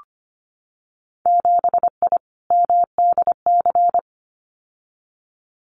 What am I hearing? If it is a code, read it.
Morse code, heard as a keyed tone, about 25 wpm, 700 Hz. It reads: 7I MDC
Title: Winter Field Day Exchanges Encore 25wpm